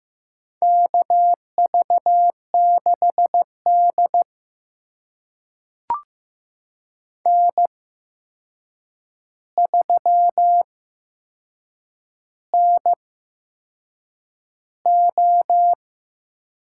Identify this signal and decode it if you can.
Morse code, heard as a keyed tone, about 15 wpm, spaced as for 6 wpm, 700 Hz. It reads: KV6D N3NO
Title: USA Call Signs - 4 Characters - 8x Character Spacing - 15wpm